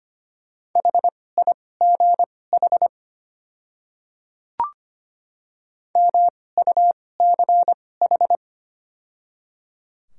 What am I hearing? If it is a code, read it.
Morse code, heard as a keyed tone, about 25 wpm, spaced as for 18 wpm, 690 Hz. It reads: HIGH MUCH
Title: 2x Character Spacing - Top 200 Words - 25wpm